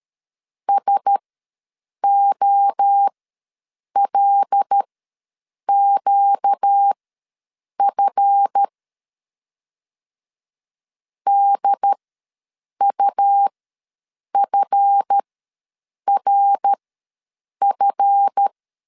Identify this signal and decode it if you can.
Morse code, heard as a keyed tone, about 13 wpm, spaced as for 7 wpm, 800 Hz. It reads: SOLQF DUFRF
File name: LEZIONI / 7 GF9Z / 7 GF ws 12wpm 3140